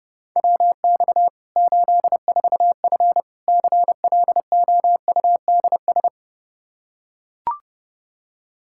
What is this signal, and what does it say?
Morse code, heard as a keyed tone, about 30 wpm, 700 Hz. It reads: WX 84F CLOUDS
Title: Practice QSO - Exchange 2 - Info - 30wpm